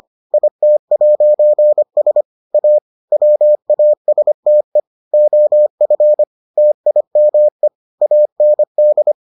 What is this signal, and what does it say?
Morse code, heard as a keyed tone, about 25 wpm, 600 Hz. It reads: IT'S A WASTE OF TIME AND